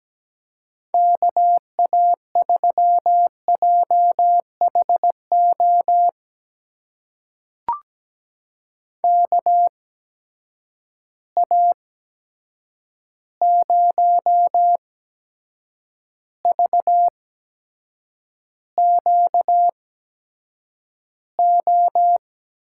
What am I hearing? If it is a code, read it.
Morse code, heard as a keyed tone, about 17 wpm, spaced as for 7 wpm, 700 Hz. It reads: KA3JHO KA0VQO
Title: USA Call Signs - 6 Characters - 8x Character Spacing - 17wpm